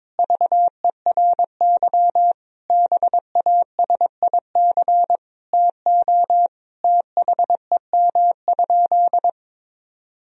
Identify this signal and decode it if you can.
Morse code, heard as a keyed tone, about 22 wpm, 700 Hz. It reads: VERY BASIC TO THEM?
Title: Sentences from Top 1000 Words - Review Part 3 - 22wpm